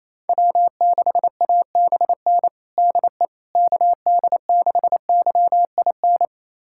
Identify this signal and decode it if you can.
Morse code, heard as a keyed tone, about 28 wpm, 700 Hz. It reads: W6ABN DE KD6YIN
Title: USA Callsign DE Callsign 28wpm